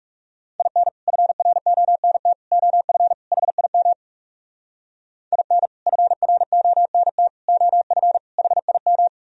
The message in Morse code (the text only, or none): IN FRONT OF HIM IN FRONT OF HIM